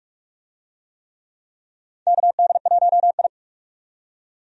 Morse code, heard as a keyed tone, about 46 wpm, 700 Hz. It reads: KD1I
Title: USA Call Signs - 4 Characters - 45wpm